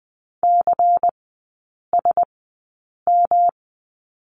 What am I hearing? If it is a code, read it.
Morse code, heard as a keyed tone, about 20 wpm, spaced as for 8 wpm, 700 Hz. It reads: CSM
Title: 138 Lesson 23e - Letter f - Three Characters 20wpm